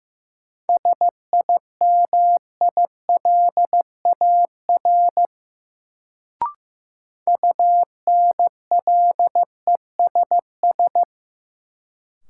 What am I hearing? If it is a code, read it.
Morse code, heard as a keyed tone, about 15 wpm, 700 Hz. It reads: SIMILAR UNLESS
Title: Sentences from Top 1000 Words - Review Part 5 - New Word Drill - 15wpm